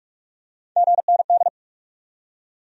Morse code, heard as a keyed tone, about 37 wpm, 700 Hz. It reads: GND
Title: Speed-Racing Abbreviations - 30wpm